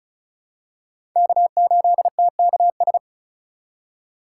Morse code, heard as a keyed tone, about 34 wpm, 700 Hz. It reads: K8TKS
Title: USA Call Signs - 5 Characters - 3x Character Spacing - 35wpm